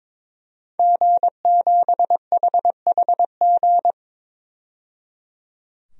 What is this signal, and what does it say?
Morse code, heard as a keyed tone, about 22 wpm, 700 Hz. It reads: G7HHG
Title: DXCC-100 Call Signs - 5 Characters - 22wpm